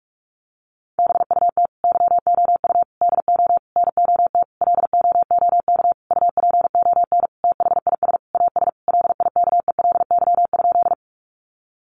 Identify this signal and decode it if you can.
Morse code, heard as a keyed tone, about 44 wpm, 700 Hz. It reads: BUT YOU DO NOT LOOK UPON THIS AS LIKELY?
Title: Sentences from Top 1100 Words 45wpm